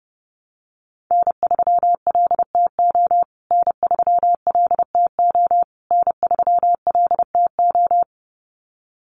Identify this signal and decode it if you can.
Morse code, heard as a keyed tone, about 30 wpm, 700 Hz. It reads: N3LTO N3LTO N3LTO
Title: USA Call Signs Repeated 3x 30wpm